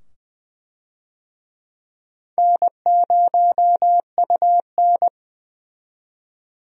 Morse code, heard as a keyed tone, about 20 wpm, 700 Hz. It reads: N0UN
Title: USA Call Signs - 4 Characters - 20wpm